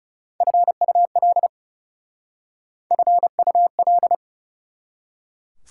Morse code, heard as a keyed tone, about 40 wpm, 690 Hz. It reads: FUL FUL